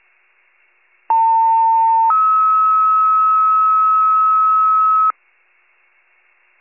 Two-tone page, heard as a one-second tone, then a three-second tone, about 900 then 1290 Hz. It is heard from 1.1 to 5.1 s.